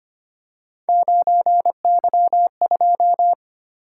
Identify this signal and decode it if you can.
Morse code, heard as a keyed tone, about 25 wpm, 700 Hz. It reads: 9Y2
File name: Three Letter-Number Combinations - 25wpm